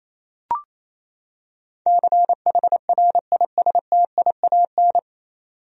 Morse code, heard as a keyed tone, about 28 wpm, 700 Hz. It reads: CHRISTIAN
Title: USA Common Names - 28wpm